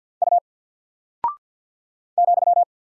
Morse code, heard as a keyed tone, about 48 wpm, 700 Hz. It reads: U ,
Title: Single Character (Letter:Number:Punctuation:Prosign) 50wpm